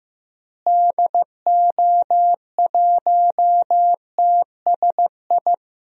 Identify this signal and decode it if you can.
Morse code, heard as a keyed tone, about 15 wpm, 700 Hz. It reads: DO1TSI
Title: DXCC-100 Call Signs - 6 Characters - Repeated 3x - 15wpm